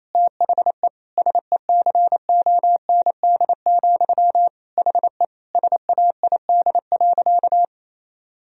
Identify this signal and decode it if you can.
Morse code, heard as a keyed tone, about 27 wpm, 700 Hz. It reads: THE SECOND, HE SAID.
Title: Sentences from Top 1000 Words - Review Part 1 - 28wpm